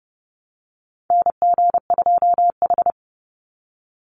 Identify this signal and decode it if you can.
Morse code, heard as a keyed tone, about 30 wpm, 700 Hz. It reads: NG2H
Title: Call Signs - 30wpm